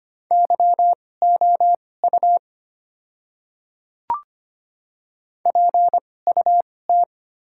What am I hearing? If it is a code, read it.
Morse code, heard as a keyed tone, about 25 wpm, spaced as for 18 wpm, 700 Hz. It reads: YOU PUT
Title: 2x Character Spacing - Top 200 Words - 25wpm